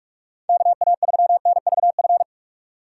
Morse code, heard as a keyed tone, about 46 wpm, 700 Hz. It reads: KA3NVF
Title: USA Call Signs - 6 Characters - 3x Character Spacing - 45wpm